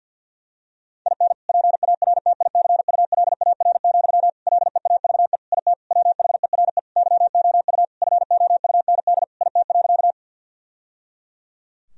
Morse code, heard as a keyed tone, about 50 wpm, 700 Hz. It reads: IN PARTICULAR, LEAVE IT WHERE YOU FOUND IT.
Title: Sentences from Top 1000 Words - Review Part 4 - 50wpm